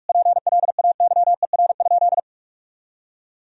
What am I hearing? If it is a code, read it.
Morse code, heard as a keyed tone, about 47 wpm, 700 Hz. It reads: PLAYER?
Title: Sentences from Top 1000 Words - Review Part 4 - 45wpm